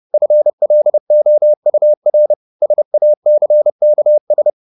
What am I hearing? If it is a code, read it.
Morse code, heard as a keyed tone, about 30 wpm, 600 Hz. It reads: FLOUR SACKS
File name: the old man and the sea / the old man and the sea 1 30wpm